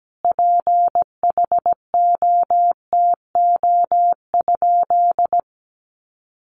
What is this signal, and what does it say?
Morse code, heard as a keyed tone, about 17 wpm, 700 Hz. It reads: PHOTO?
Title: Sentences from Top 1200 Words 17wpm